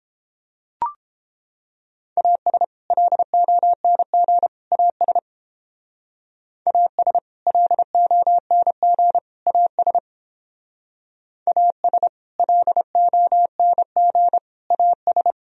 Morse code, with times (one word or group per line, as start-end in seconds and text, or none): 2.2-2.6 AS
2.9-4.5 LONG
4.7-5.2 AS
6.7-7.2 AS
7.5-9.2 LONG
9.5-10.0 AS
11.5-12.1 AS
12.4-14.4 LONG
14.7-15.3 AS